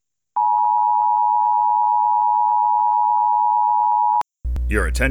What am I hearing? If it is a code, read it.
Emergency Alert System (SAME) tones (the 853 and 960 Hz attention tone), with speech. It is heard from 0.3 to 4.3 s.
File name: 2022-Severe-Weather-Take-Shelter-Message-Parking